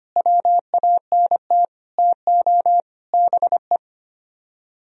Morse code, heard as a keyed tone, about 32 wpm, 700 Hz. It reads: WANT TO BE